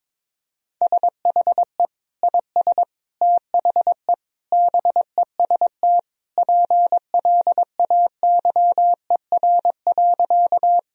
Morse code, heard as a keyed tone, about 22 wpm, 700 Hz. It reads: SHE IS THE BEST PLAYER.
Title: Sentences from Top 1000 Words - Review Part 4 - 22wpm